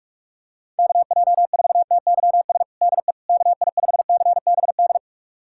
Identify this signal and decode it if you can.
Morse code, heard as a keyed tone, about 41 wpm, 700 Hz. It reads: KJ4TYS DE KI5CBD
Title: USA Callsign DE Callsign - 6 Characters - 45wpm